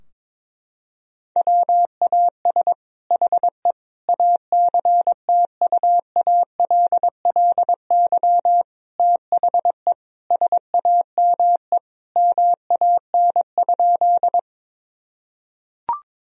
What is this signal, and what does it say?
Morse code, heard as a keyed tone, about 22 wpm, 700 Hz. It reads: WAS HE ACTUALLY THE SAME MAN?
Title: Sentences from Top 1000 Words - Review Part 4 - 22wpm